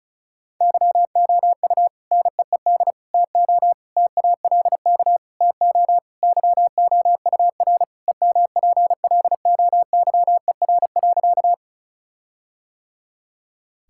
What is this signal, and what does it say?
Morse code, heard as a keyed tone, about 36 wpm, 700 Hz. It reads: YOU NEED TO TALK TO YOUR EMPLOYER.
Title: Sentences from Top 1600 Words 35wpm